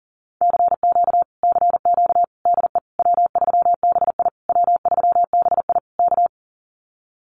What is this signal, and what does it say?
Morse code, heard as a keyed tone, about 40 wpm, 700 Hz. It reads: CQ CQ DE W3BI W3BI K
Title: Common QSO Phrases Encore 40wpm